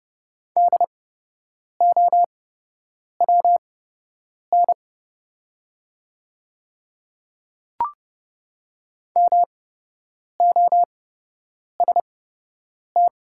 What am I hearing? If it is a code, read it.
Morse code, heard as a keyed tone, about 30 wpm, spaced as for 8 wpm, 700 Hz. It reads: DOWN MOST
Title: 8x Character Spacing - Top 200 Words - 30wpm